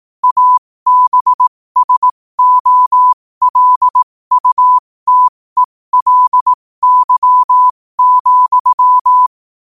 Morse code, heard as a keyed tone, about 18 wpm, spaced as for 15 wpm, 1000 Hz. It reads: ABSOLUTELY,
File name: qotdfiles / 15wpm